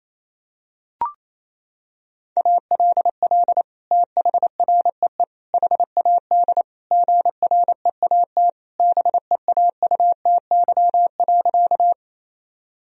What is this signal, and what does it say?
Morse code, heard as a keyed tone, about 28 wpm, 700 Hz. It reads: ALL THREE HAD GREAT BEAUTY.
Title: Sentences from Top 1000 Words - Review Part 2 - 28wpm